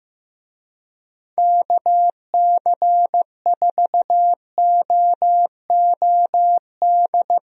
Morse code, heard as a keyed tone, about 14 wpm, 700 Hz. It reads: KC4OOD
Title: USA Call Signs - 6 Characters - Repeated 3x - 15wpm